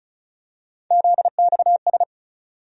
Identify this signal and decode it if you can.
Morse code, heard as a keyed tone, about 36 wpm, 700 Hz. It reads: ZXS
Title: USA Call Sign Postfixes - 3 Letters 35wpm